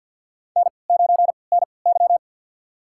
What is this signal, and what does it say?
Morse code, heard as a keyed tone, about 53 wpm, spaced as for 29 wpm, 700 Hz. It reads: N9NY
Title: USA Call Signs - 3x Character Spacing - 50wpm